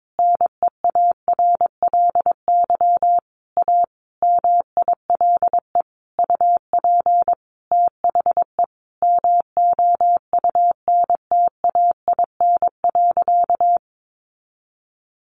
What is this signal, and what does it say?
Morse code, heard as a keyed tone, about 22 wpm, 700 Hz. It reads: NEARLY A MILE UP THE MOUNTAIN.
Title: Sentences from Top 1300 Words 22wpm